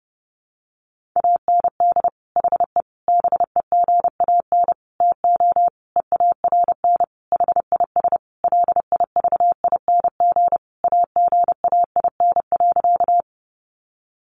Morse code, heard as a keyed tone, about 30 wpm, 700 Hz. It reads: AND HE BEGAN TO EARN HIS LIVING AGAIN.
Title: Sentences from Top 1500 Words 30wpm